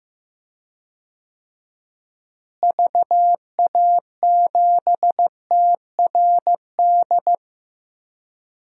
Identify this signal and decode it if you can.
Morse code, heard as a keyed tone, about 16 wpm, 700 Hz. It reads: VA7TRD
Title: DXCC-100 Call Signs - 6 Characters - Repeated 2x - 15wpm